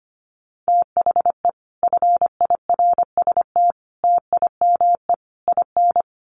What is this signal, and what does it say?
Morse code, heard as a keyed tone, about 25 wpm, 700 Hz. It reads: THE FIRST TIME IN
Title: Sets of 4 Words 25wpm